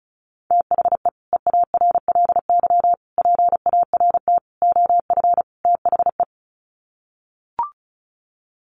Morse code, heard as a keyed tone, about 34 wpm, 700 Hz. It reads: THE EARLY PART OF THE
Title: Sets of 5 Words Encore2 35wpm